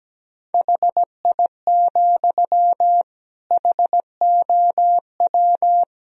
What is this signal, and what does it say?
Morse code, heard as a keyed tone, about 17 wpm, 700 Hz. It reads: HI, HOW
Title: Sentences from Top 1000 Words - Review Part 3 - 17wpm